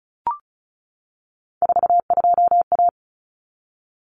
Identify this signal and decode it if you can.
Morse code, heard as a keyed tone, about 34 wpm, 700 Hz. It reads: 42A
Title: Common Cut Numbers - 3-Digit 35wpm